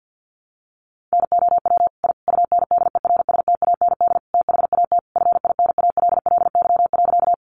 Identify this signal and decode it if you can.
Morse code, heard as a keyed tone, about 50 wpm, 700 Hz. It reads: NOW I UNDERSTAND THAT FINALLY.
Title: Sentences from Top 1000 Words 50wpm